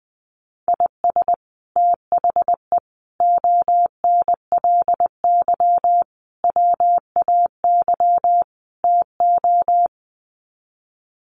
Morse code, heard as a keyed tone, about 20 wpm, 700 Hz. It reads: IS THE ONLY WAY TO
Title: Sets of 5 Words Encore 20wpm